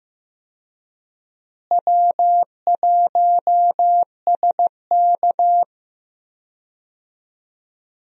Morse code, heard as a keyed tone, about 15 wpm, 700 Hz. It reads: W1SK
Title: USA Call Signs - 4 Characters - Repeated 3x - 15wpm